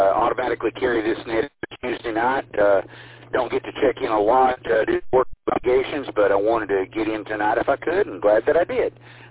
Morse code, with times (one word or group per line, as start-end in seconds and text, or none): none